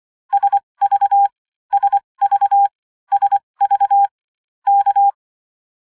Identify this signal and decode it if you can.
Morse code, heard as a keyed tone, about 25 wpm, spaced as for 20 wpm, 800 Hz. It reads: SV SV SV K
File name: Radio beacon Sierra Victor.ogg / revision / latest